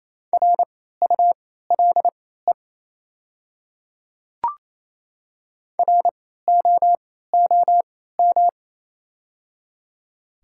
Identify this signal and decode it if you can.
Morse code, heard as a keyed tone, about 28 wpm, spaced as for 16 wpm, 700 Hz. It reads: RULE ROOM
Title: 3x Character Spacing - Top 500 Words - 28wpm